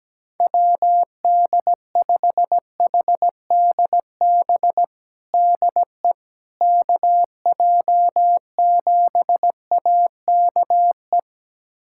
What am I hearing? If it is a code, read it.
Morse code, heard as a keyed tone, about 17 wpm, 700 Hz. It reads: WD5HDB DE KJ7AKE
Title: USA Callsign DE Callsign - 6 Characters - 17wpm